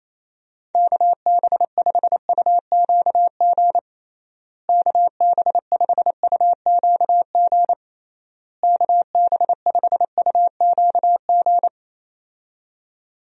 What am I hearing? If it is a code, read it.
Morse code, heard as a keyed tone, about 28 wpm, 700 Hz. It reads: KB5UQG KB5UQG KB5UQG
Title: USA Call Signs - 6 Characters - Repeated 3x - 28wpm